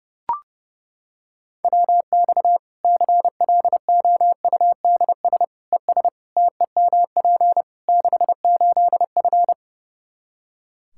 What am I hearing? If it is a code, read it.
Morse code, heard as a keyed tone, about 29 wpm, 700 Hz. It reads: WX CLOUDS ES TEMP 68F
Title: Practice QSO - Exchange 2 - Info - 30wpm